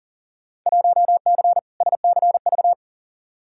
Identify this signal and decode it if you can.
Morse code, heard as a keyed tone, about 40 wpm, 700 Hz. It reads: 1C SCV
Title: Field Day Exchanges - 40wpm